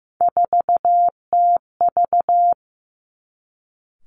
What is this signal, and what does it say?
Morse code, heard as a keyed tone, about 15 wpm, 700 Hz. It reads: 4TV
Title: Three Challenging Letter-Number Combinations 15wpm